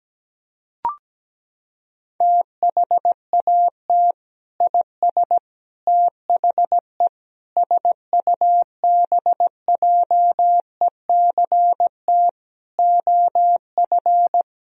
Morse code, heard as a keyed tone, about 17 wpm, 700 Hz. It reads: THAT IS THE SUBJECT OF
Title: Sentences from Top 1000 Words - Review Part 4 - 17wpm